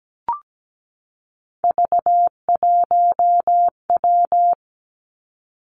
Morse code, heard as a keyed tone, about 17 wpm, 700 Hz. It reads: V1W
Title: Three Challenging Letter-Number Combinations 17wpm